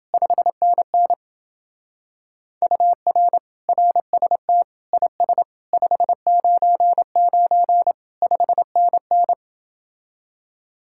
Morse code, heard as a keyed tone, about 29 wpm, 700 Hz. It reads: 5NN UR RST IS 599 5NN